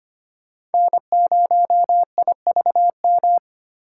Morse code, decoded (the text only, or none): N0IVM